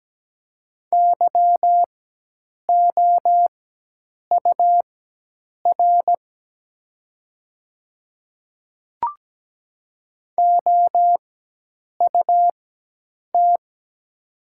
Morse code, heard as a keyed tone, about 17 wpm, spaced as for 8 wpm, 700 Hz. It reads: YOUR OUT
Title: 4x Character Spacing - Top 100 Words - 17wpm